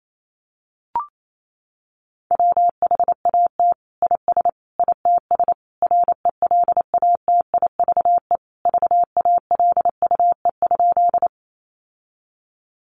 Morse code, heard as a keyed tone, about 28 wpm, 700 Hz. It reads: WHAT IS ITS RELATIVE VALUE?
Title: Sentences from Top 1400 Words 28wpm